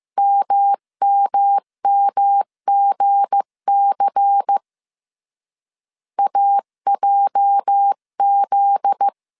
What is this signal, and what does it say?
Morse code, heard as a keyed tone, about 15 wpm, 800 Hz. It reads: MMMGC AJZ
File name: CW / DET / 1 9 1500C S B 14wpm 3302